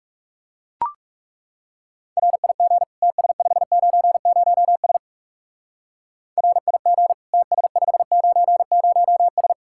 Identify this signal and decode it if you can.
Morse code, heard as a keyed tone, about 37 wpm, 700 Hz. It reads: RIG TS590S RIG TS590S
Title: Speed-Racing Common QSO Phrases Encore - 30wpm